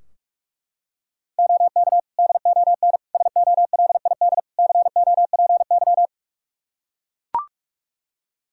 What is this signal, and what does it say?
Morse code, heard as a keyed tone, about 42 wpm, 700 Hz. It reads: OK DON SOLID COPY
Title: Practice QSO - Exchange 3 - More Info - 45wpm